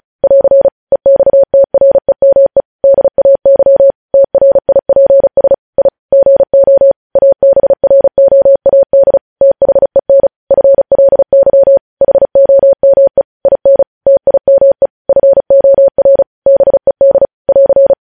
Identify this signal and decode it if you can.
Morse code, heard as a keyed tone, about 36 wpm, 550 Hz. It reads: <AR> EXTREME DAY TRIPS I GO ABROAD THEN FLY HOME IN TIME FOR BED <AR>
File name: podcast / News headlines-1x-35x35